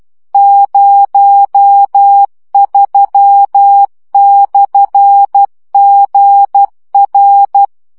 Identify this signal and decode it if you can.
Morse code, heard as a keyed tone, about 12 wpm, 800 Hz. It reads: 03/GR
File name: morze vizsga 2011-3 10